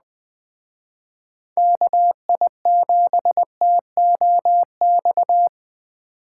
Morse code, heard as a keyed tone, about 20 wpm, 700 Hz. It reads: KI7TOX